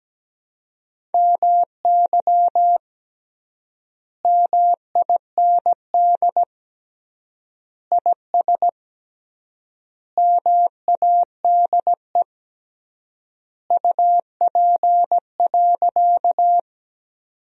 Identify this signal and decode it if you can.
Morse code, heard as a keyed tone, about 17 wpm, 700 Hz. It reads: MY MIND IS MADE UP.
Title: 3x Word-Spacing - Sentences from Top 500 Words - 17wpm